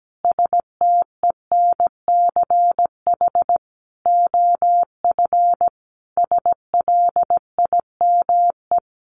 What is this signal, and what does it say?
Morse code, heard as a keyed tone, about 17 wpm, 700 Hz. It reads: STENCH OF SLIME
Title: A Sound Of Thunder 17wpm